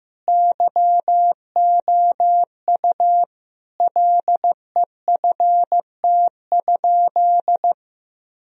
Morse code, heard as a keyed tone, about 15 wpm, 700 Hz. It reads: YOU LEFT?
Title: Sentences from Top 1000 Words - Review Part 2 - 15wpm